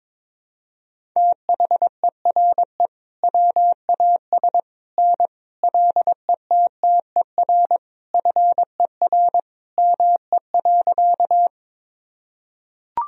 Morse code, heard as a keyed tone, about 22 wpm, 700 Hz. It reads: THERE WAS N LETTER FER ME.
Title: Sentences from Top 300 Words and Abbreviations 22wpm